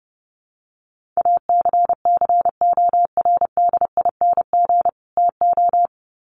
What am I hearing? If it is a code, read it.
Morse code, heard as a keyed tone, about 30 wpm, 700 Hz. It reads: ACCORDING TO
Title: Sets of 2 Words 30wpm